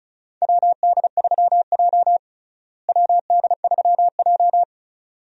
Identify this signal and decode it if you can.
Morse code, heard as a keyed tone, about 36 wpm, 700 Hz. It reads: WD3J WD3J
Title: USA Call Signs - 4 Characters - Repeated 2x - 35wpm